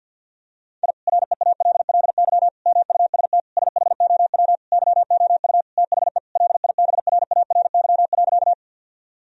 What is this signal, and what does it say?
Morse code, heard as a keyed tone, about 52 wpm, 700 Hz. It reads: I REALLY MUST SHOW YOU THE LIBRARY.
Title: Sentences from Top 1000 Words - Review Part 4 - 50wpm